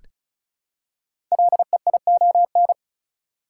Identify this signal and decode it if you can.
Morse code, heard as a keyed tone, about 36 wpm, 690 Hz. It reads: LEION
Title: Code Groups - 5 Letter:Numbers - 35wpm